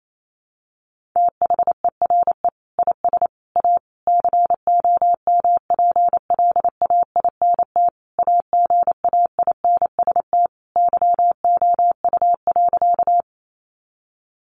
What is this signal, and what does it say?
Morse code, heard as a keyed tone, about 28 wpm, 700 Hz. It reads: THERE IS A COMPLAINT AGAINST YOU.
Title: Sentences from Top 1400 Words 28wpm